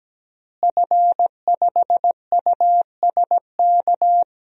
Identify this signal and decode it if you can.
Morse code, heard as a keyed tone, about 17 wpm, 700 Hz. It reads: F5USK
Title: DXCC-100 Call Signs - 5 Characters - Repeated 2x - 17wpm